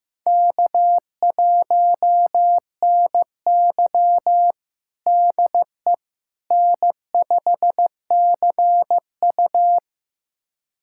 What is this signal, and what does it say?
Morse code, heard as a keyed tone, about 15 wpm, 700 Hz. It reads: K1NY DE N5CU
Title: USA Callsign DE Callsign - 4 Characters - 15wpm